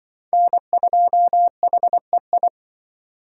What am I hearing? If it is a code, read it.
Morse code, heard as a keyed tone, about 23 wpm, 700 Hz. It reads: N2HEI
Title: Speed Racing - USA 1x3 Call Signs 20wpm